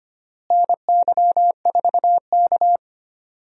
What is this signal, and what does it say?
Morse code, heard as a keyed tone, about 25 wpm, 700 Hz. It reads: NY4K
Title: USA Call Signs - 4 Characters - Repeated 3x - 25wpm